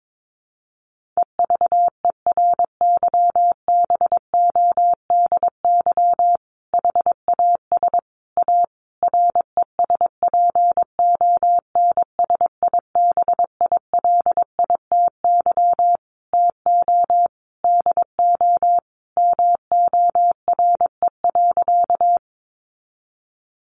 Morse code, heard as a keyed tone, about 22 wpm, 700 Hz. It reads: EVERYBODY HAS A RESPONSIBILITY TO DO MORE.
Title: Sentences from Top 1400 Words 22wpm